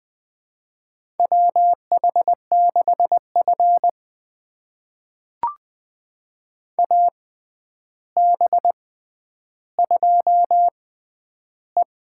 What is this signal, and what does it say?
Morse code, heard as a keyed tone, about 20 wpm, spaced as for 10 wpm, 700 Hz. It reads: WH6F AB2E
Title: USA Call Signs - 4 Characters - 6x Character Spacing - 20wpm